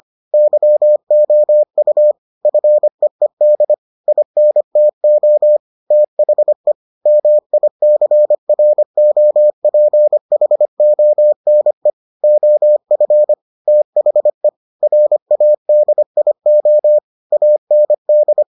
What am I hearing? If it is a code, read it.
Morse code, heard as a keyed tone, about 25 wpm, 600 Hz. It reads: YOU FEED INTO THE MICROPHONE OF THE RADIO AND